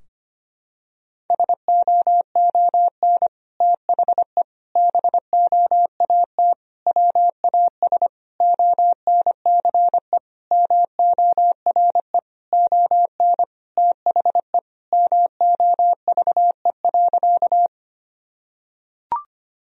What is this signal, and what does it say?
Morse code, heard as a keyed tone, about 25 wpm, 700 Hz. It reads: SOON THE BOAT WAS ONCE MORE ON THE MOVE.
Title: Sentences from Top 1000 Words - Review Part 3 - 25wpm